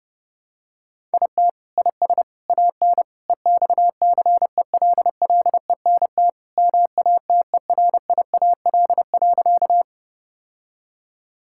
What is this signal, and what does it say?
Morse code, heard as a keyed tone, about 32 wpm, 700 Hz. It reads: IT IS AN EXCELLENT MATERIAL.
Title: Sentences from Top 1000 Words - Review Part 5 - 30wpm